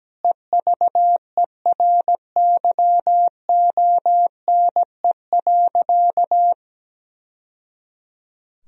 Morse code, heard as a keyed tone, about 17 wpm, 700 Hz. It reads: EVERYONE.
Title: Sentences from Top 1000 Words - Review Part 3 - 17wpm